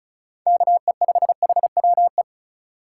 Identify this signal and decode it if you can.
Morse code, heard as a keyed tone, about 36 wpm, 690 Hz. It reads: KE5HWE
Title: USA Call Signs - 6 Characters - 2x Character Spacing - 35wpm